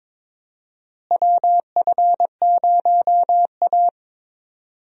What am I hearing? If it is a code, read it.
Morse code, heard as a keyed tone, about 22 wpm, 700 Hz. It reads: WF0A
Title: USA Call Signs - 4 Characters - Repeated 2x - 22wpm